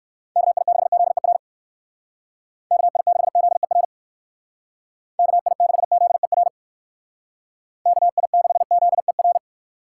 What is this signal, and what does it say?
Morse code, heard as a keyed tone, about 56 wpm, 700 Hz. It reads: KI6ZER KI6ZER KI6ZER KI6ZER